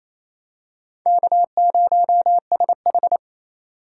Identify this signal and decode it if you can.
Morse code, heard as a keyed tone, about 28 wpm, 700 Hz. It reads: K0SH
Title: USA Call Signs - 4 Characters - 4x Character Spacing - 28wpm